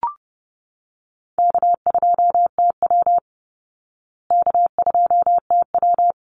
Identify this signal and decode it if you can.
Morse code, heard as a keyed tone, about 30 wpm, 700 Hz. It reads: K2TW K2TW
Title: Easy Call Signs Repeated 3x - 30wpm